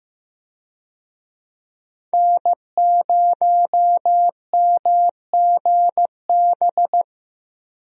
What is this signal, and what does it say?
Morse code, heard as a keyed tone, about 15 wpm, 700 Hz. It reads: N0MGB
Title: USA Call Signs - 5 Characters - Repeated 3x - 15wpm